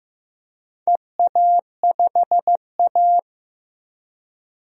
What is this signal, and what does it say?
Morse code, heard as a keyed tone, about 15 wpm, 700 Hz. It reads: EA5A
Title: DXCC-100 Call Signs - 4 Characters - 15wpm